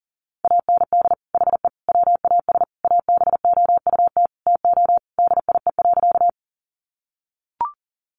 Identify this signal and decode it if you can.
Morse code, heard as a keyed tone, about 37 wpm, 700 Hz. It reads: AND HE WAS ABOUT TO DIE.
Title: Sentences from Top 1300 Words 40wpm